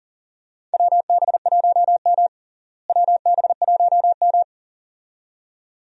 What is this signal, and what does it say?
Morse code, heard as a keyed tone, about 40 wpm, 700 Hz. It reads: WB1M WB1M
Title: USA Call Signs - 4 Characters - Repeated 2x - 40wpm